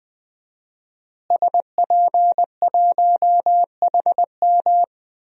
Morse code, heard as a keyed tone, about 20 wpm, 700 Hz. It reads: SP1HM
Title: DXCC-100 Call Signs - 5 Characters - Repeated 2x - 20wpm